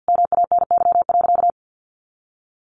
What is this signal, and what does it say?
Morse code, heard as a keyed tone, about 50 wpm, 700 Hz. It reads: MANY.
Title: Sentences from Top 1300 Words 50wpm